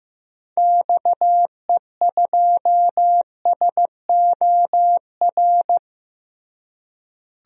Morse code, heard as a keyed tone, about 15 wpm, 700 Hz. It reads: XE2SOR
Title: DXCC-100 Call Signs - 6 Characters - Repeated 2x - 15wpm